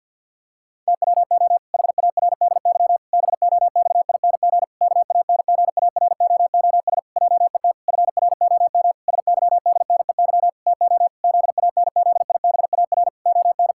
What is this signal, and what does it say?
Morse code, heard as a keyed tone, about 50 wpm, 700 Hz. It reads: TWO HARDY BOXING KANGAROOS JET FROM SYDNEY TO ZANZIBAR ON
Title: Pangram Challenge v2 - 50wpm